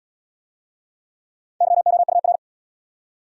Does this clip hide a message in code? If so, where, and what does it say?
Yes: Morse code from 1.6 to 2.4 s, heard as a keyed tone, about 74 wpm, 700 Hz. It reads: YOUR